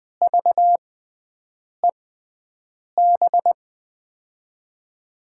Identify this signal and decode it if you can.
Morse code, heard as a keyed tone, about 21 wpm, 700 Hz. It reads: V E B
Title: USA Call Signs - 6 Characters - 6x Character Spacing - 20wpm